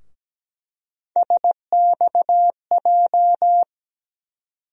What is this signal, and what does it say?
Morse code, heard as a keyed tone, about 17 wpm, 700 Hz. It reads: SXJ
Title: USA Call Sign Postfixes - 3 Letters 17wpm